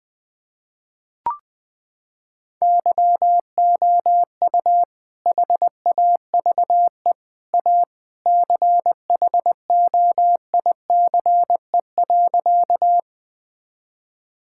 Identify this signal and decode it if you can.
Morse code, heard as a keyed tone, about 21 wpm, 700 Hz. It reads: YOU HAVE A CHOICE.
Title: Sentences with 3-Word Phrases - 20wpm